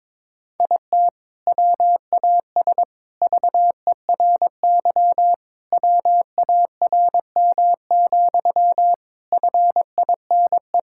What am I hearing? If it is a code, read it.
Morse code, heard as a keyed tone, about 22 wpm, 700 Hz. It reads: IT WAS VERY WARM, FINE
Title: Sentences from Top 1000 Words - Review Part 5 - 22wpm